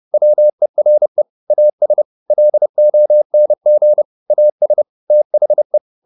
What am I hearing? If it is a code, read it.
Morse code, heard as a keyed tone, about 30 wpm, 600 Hz. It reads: WERE AS LONG AS THE
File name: the old man and the sea / the old man and the sea 1 30wpm